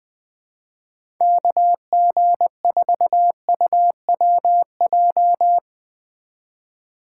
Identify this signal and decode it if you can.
Morse code, heard as a keyed tone, about 20 wpm, 700 Hz. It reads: KG4UWJ
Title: USA Call Signs - 6 Characters - 8x Character Spacing - 20wpm